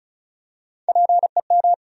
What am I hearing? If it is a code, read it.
Morse code, heard as a keyed tone, about 36 wpm, 700 Hz. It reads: PEM
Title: USA Call Sign Postfixes - 3 Letters 35wpm